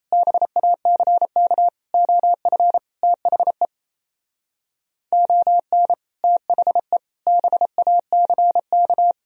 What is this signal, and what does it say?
Morse code, heard as a keyed tone, about 35 wpm, 700 Hz. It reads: BACK OF THE ON THE BACK